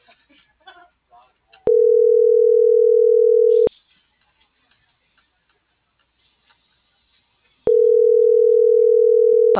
Ringback tone (North America), 440 + 480 Hz.
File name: CENTRAL MAERSK / out isabel.bravo - .82685